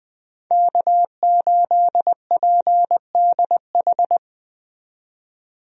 Morse code, heard as a keyed tone, about 20 wpm, 700 Hz. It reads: K8PDH
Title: USA Call Signs - 5 Characters - 20wpm